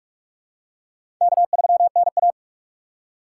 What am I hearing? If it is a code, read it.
Morse code, heard as a keyed tone, about 45 wpm, 700 Hz. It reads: K3NA
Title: USA Call Signs - 4 Characters - 2x Character Spacing - 45wpm